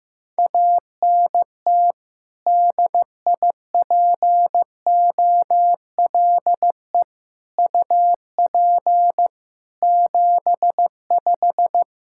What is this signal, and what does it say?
Morse code, heard as a keyed tone, about 15 wpm, 700 Hz. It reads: ANT DIPOLE UP 75
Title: K1USN SST QSO Practice 15wpm